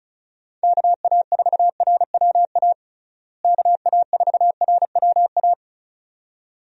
Morse code, heard as a keyed tone, about 34 wpm, 700 Hz. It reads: KA4RWA KA4RWA
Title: USA Call Signs - 6 Characters - Repeated 2x - 35wpm